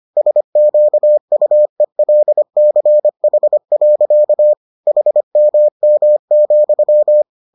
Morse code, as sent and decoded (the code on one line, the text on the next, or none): ... --.- ..- . .-.. -.-. .... .-.-.- / .... -- -- --..--
SQUELCH. HMM,